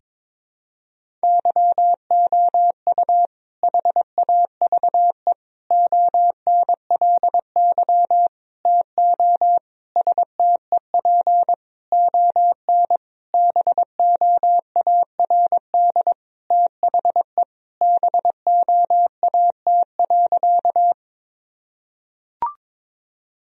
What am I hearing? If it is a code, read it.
Morse code, heard as a keyed tone, about 22 wpm, 700 Hz. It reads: YOU HAVE ONLY TO STEP ON BOARD THE BOAT.
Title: Sentences from Top 1000 Words - Review Part 4 - 22wpm